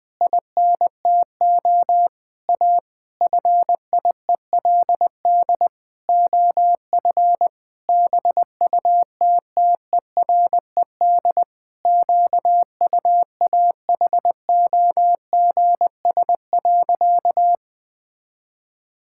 Morse code, heard as a keyed tone, about 20 wpm, 700 Hz. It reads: INTO A FIELD OF BUTTERED QUAHOGS.
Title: Pangram Challenge v2 - 20wpm